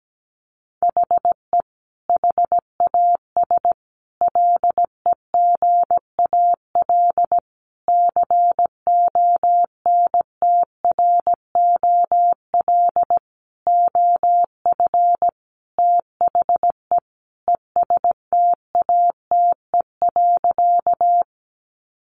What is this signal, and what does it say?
Morse code, heard as a keyed tone, about 17 wpm, 700 Hz. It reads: HE HAS LEGAL CONTROL OF THE ESTATE.
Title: Sentences from Top 1200 Words 17wpm